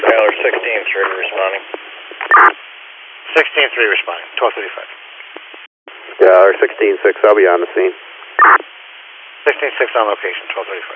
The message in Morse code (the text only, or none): none